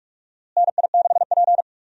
Morse code, heard as a keyed tone, about 46 wpm, 700 Hz. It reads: NI6P
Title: USA Call Signs - 4 Characters - Repeated 2x - 45wpm